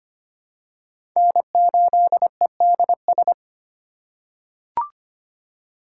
Morse code, heard as a keyed tone, about 25 wpm, 700 Hz. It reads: N8EDS
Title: USA Call Signs - 5 Characters - 6x Character Spacing - 25wpm